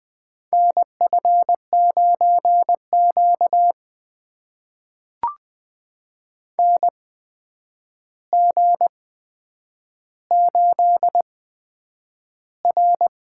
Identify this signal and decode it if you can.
Morse code, heard as a keyed tone, about 20 wpm, spaced as for 9 wpm, 700 Hz. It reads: NF9Q NG8R
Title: USA Call Signs - 4 Characters - 8x Character Spacing - 20wpm